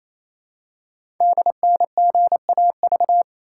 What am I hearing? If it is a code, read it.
Morse code, heard as a keyed tone, about 28 wpm, 700 Hz. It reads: DNGAV
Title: Code Groups - 5 Letters - 28wpm